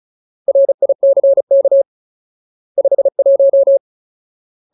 Morse code, heard as a keyed tone, about 34 wpm, 550 Hz. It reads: RICK 51
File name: Ultimate Word List / 35WPM Ultimate